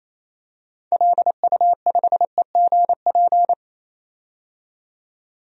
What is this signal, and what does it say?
Morse code, heard as a keyed tone, about 27 wpm, 700 Hz. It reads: LU5EGP
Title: DXCC-100 Call Signs - 6 Characters - 28wpm